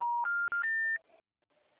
Special information tone.